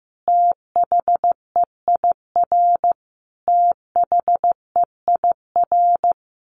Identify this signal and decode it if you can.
Morse code, heard as a keyed tone, about 15 wpm, 700 Hz. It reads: THEIR THEIR
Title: Top 500 Words Repeated 15wpm